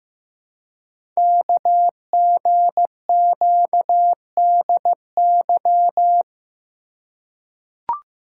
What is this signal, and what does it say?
Morse code, heard as a keyed tone, about 15 wpm, 700 Hz. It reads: KGQDY
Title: Code Groups - 5 Letters - 15wpm